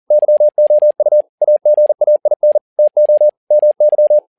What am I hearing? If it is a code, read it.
Morse code, heard as a keyed tone, about 40 wpm, 600 Hz. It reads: YOU AGAIN TO MY
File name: learning center / tower trouble / tower touble 40wpm